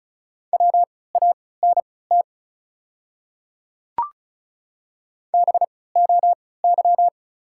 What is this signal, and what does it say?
Morse code, heard as a keyed tone, about 36 wpm, spaced as for 20 wpm, 700 Hz. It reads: WANT BOY
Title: 3x Character Spacing - Top 300 Words - 35wpm